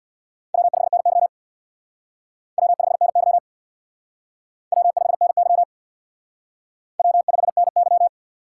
Morse code, heard as a keyed tone, about 64 wpm, 700 Hz. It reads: W5NY W5NY W5NY W5NY